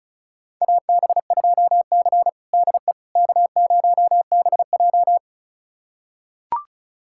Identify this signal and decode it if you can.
Morse code, heard as a keyed tone, about 36 wpm, 700 Hz. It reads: AB2C DE K0BJ
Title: USA Callsign DE Callsign - 4 Characters - 35wpm